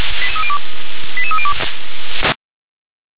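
Five-tone selective call (ZVEI), heard as quick runs of five tones, tones of about 70 ms. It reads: 88302
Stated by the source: ZVEI five-tone selective call received via SDR.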